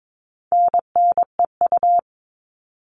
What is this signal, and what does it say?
Morse code, heard as a keyed tone, about 22 wpm, 700 Hz. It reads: NNEU
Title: All Cut Numbers - 4-Digit 22wpm